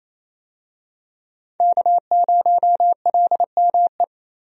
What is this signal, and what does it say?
Morse code, heard as a keyed tone, about 28 wpm, 700 Hz. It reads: K0LME
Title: USA Call Signs - 5 Characters - Repeated 2x - 28wpm